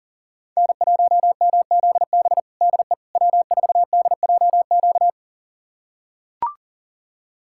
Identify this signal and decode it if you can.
Morse code, heard as a keyed tone, about 40 wpm, 700 Hz. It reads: N1MZB DE W4DJQ
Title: USA Callsign DE Callsign - 5 Characters - 40wpm